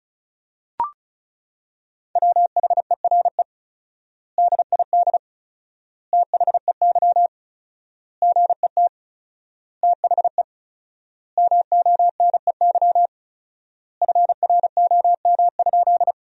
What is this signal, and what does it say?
Morse code, heard as a keyed tone, about 34 wpm, 700 Hz. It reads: WHERE DID THEY GET THE MONEY FROM?
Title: 4x Word-Spacing - Sentences from Top 400 Words - 35wpm